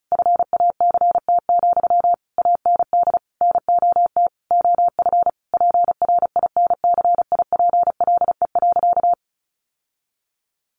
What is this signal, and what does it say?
Morse code, heard as a keyed tone, about 34 wpm, 700 Hz. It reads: FACT, AND NOT OF PRINCIPLE.
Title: Sentences from Top 1400 Words 35wpm